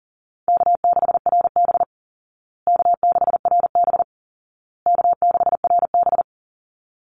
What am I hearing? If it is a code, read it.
Morse code, heard as a keyed tone, about 40 wpm, 700 Hz. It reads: K6RB K6RB K6RB
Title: Easy Call Signs Repeated 3x - 40wpm